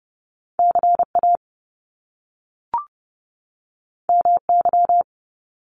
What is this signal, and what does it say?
Morse code, heard as a keyed tone, about 30 wpm, 700 Hz. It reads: CA MY
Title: Top Two Letter Combinations in Words 30wpm